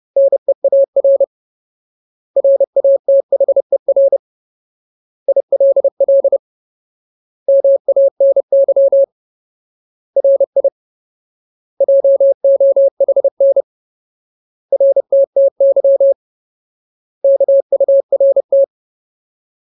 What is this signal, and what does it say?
Morse code, heard as a keyed tone, about 30 wpm, 550 Hz. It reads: NEAR RATHER ILL MANY RI JOHN RTTY KURT
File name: Ultimate Word List / 30WPM Ultimate